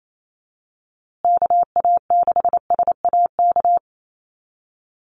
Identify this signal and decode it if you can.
Morse code, heard as a keyed tone, about 28 wpm, 700 Hz. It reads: KA6SAK
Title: USA Call Signs Repeated 2x 28wpm